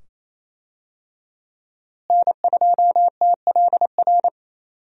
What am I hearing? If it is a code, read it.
Morse code, heard as a keyed tone, about 28 wpm, 700 Hz. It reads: N2TLR
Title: USA Call Signs - 5 Characters - Repeated 2x - 28wpm